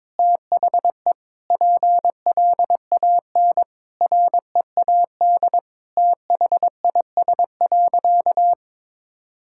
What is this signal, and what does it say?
Morse code, heard as a keyed tone, about 22 wpm, 700 Hz. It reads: THE PLAN READ THIS.
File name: practice / 22wpm / Sentences with 8-Word Phrases - 22wpm